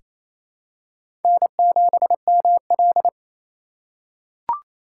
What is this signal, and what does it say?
Morse code, heard as a keyed tone, about 28 wpm, 700 Hz. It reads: N7ML
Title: K1USN SST QSO Practice - 28wpm